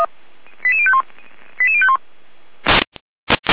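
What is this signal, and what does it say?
Five-tone selective call (ZVEI), heard as quick runs of five tones, tones of about 70 ms. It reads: 88051 88051
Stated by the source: ZVEI five-tone call 88125 received by SDR.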